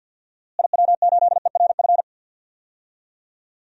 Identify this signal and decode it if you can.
Morse code, heard as a keyed tone, about 51 wpm, 700 Hz. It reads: IW8ERF
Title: DXCC-100 Call Signs - 6 Characters - Repeated 3x - 50wpm